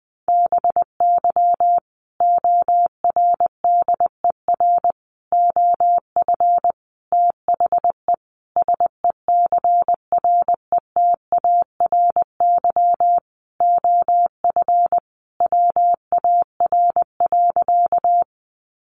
Morse code, heard as a keyed tone, about 20 wpm, 700 Hz. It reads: BY ORDER OF THE SECRETARY OF WAR.
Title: Sentences from Top 1300 Words 20wpm